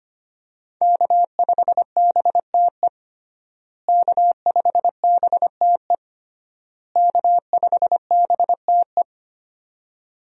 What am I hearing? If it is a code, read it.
Morse code, heard as a keyed tone, about 25 wpm, 700 Hz. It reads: K5BTE K5BTE K5BTE
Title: USA Call Signs - 5 Characters - Repeated 3x - 25wpm